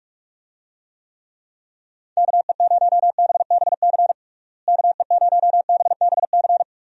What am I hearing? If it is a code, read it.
Morse code, heard as a keyed tone, about 45 wpm, 700 Hz. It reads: KE0BBC KE0BBC
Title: USA Call Signs - 6 Characters - Repeated 2x - 45wpm